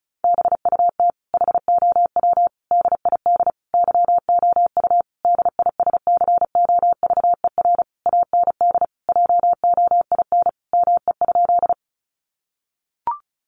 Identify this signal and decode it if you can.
Morse code, heard as a keyed tone, about 34 wpm, 700 Hz. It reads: BUT HOW DID YOU DISCOVER AND JOIN ME?
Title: Sentences from Top 1400 Words 35wpm